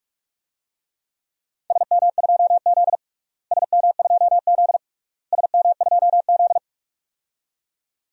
Morse code, heard as a keyed tone, about 46 wpm, 700 Hz. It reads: SM2Z SM2Z SM2Z
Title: DXCC-100 Call Signs - 4 Characters - Repeated 3x - 45wpm